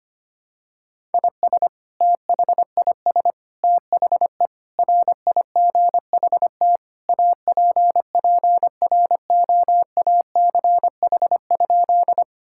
Morse code, heard as a keyed tone, about 25 wpm, 700 Hz. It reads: IS THIS THE RIGHT APPROACH?
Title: Sentences from Top 1000 Words - Review Part 6 - 25wpm